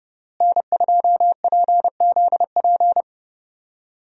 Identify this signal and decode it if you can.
Morse code, heard as a keyed tone, about 29 wpm, 700 Hz. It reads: N2PZP
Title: USA Call Signs - 5 Characters - Repeated 2x - 30wpm